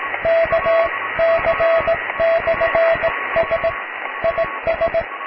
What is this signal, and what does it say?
Morse code, heard as a keyed tone, about 18 wpm, 660 Hz, with radio noise. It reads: KC/S IS
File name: cw / tekst(b)+QRM